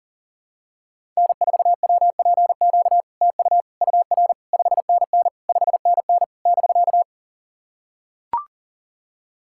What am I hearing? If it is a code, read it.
Morse code, heard as a keyed tone, about 44 wpm, 700 Hz. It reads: N4WPQ TU UR 5NN 5NN <BK>
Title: SOTA QSO Practice 40wpm